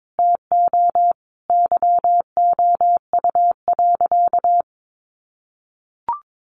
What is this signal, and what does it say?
Morse code, heard as a keyed tone, about 22 wpm, 700 Hz. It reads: TO YOU.
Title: Sentences from Top 1200 Words 22wpm